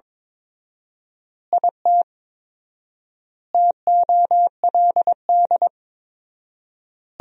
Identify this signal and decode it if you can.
Morse code, heard as a keyed tone, about 22 wpm, 700 Hz. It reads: IT TOLD
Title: 4x Word-Spacing - Sentences from Top 400 Words - 22wpm